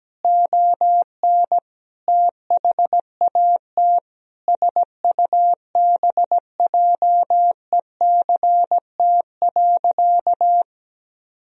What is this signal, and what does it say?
Morse code, heard as a keyed tone, about 17 wpm, 700 Hz. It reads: ON THAT SUBJECT.
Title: Sentences from Top 1000 Words - Review Part 1 - 17wpm